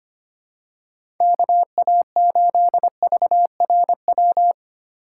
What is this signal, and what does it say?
Morse code, heard as a keyed tone, about 25 wpm, 700 Hz. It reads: KA8VRW
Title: USA Call Signs - 6 Characters - 2x Character Spacing - 25wpm